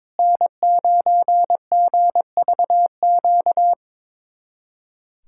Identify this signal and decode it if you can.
Morse code, heard as a keyed tone, about 22 wpm, 700 Hz. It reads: N9GVQ
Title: USA Callsign DE Callsign 22wpm